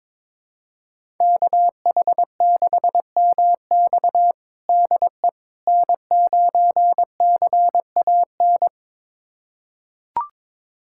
Morse code, heard as a keyed tone, about 22 wpm, 700 Hz. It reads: KH6MX DE N9CAN
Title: USA Callsign DE Callsign - 5 Characters - 22wpm